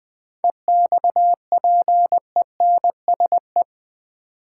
Morse code, heard as a keyed tone, about 20 wpm, 700 Hz. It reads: EXPENSE
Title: Sets of 4 Words - New Word Drill - 20wpm